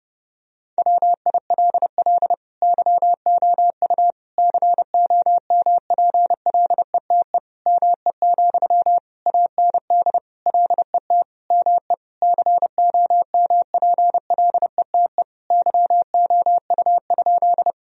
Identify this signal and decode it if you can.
Morse code, heard as a keyed tone, about 30 wpm, 700 Hz. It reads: WILL YOU COMPLETE ME, AND LET ME COMPLETE YOU?
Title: Sentences from Top 1000 Words - Review Part 2 - 30wpm